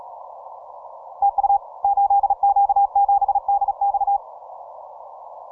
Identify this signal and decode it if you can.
Morse code, heard as a keyed tone, about 35 wpm, 800 Hz. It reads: TU 8Q7DX
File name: simple-file-list / 8Q7DX 15MCW